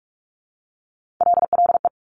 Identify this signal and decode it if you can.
Morse code, heard as a keyed tone, about 46 wpm, 700 Hz. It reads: LLE
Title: Top Three Letter Combinations in Words 45wpm